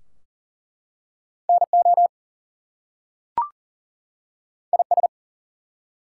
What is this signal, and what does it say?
Morse code, heard as a keyed tone, about 38 wpm, 700 Hz. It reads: NO IS
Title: Two Letter Words - 40wpm